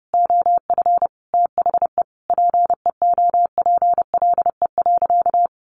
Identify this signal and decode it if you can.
Morse code, heard as a keyed tone, about 30 wpm, 700 Hz. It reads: OF THE PEOPLE.
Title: Sentences from Top 1500 Words 30wpm